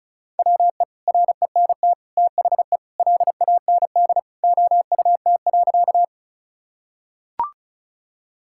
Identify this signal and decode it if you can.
Morse code, heard as a keyed tone, about 36 wpm, 700 Hz. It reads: WE RENT THE LAND OUT.
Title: Sentences from Top 1000 Words - Review Part 5 - 35wpm